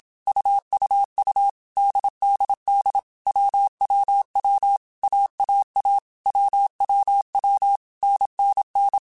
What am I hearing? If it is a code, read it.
Morse code, heard as a keyed tone, about 27 wpm, 800 Hz. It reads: UUU DDD WWW AAA WWW NNN
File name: Basic Sets / Basic Letter Groups Collective7 / AWUNDG@25